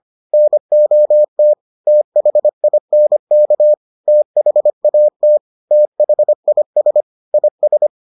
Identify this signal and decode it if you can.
Morse code, heard as a keyed tone, about 25 wpm, 600 Hz. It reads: NOT THINK THAT THIS IS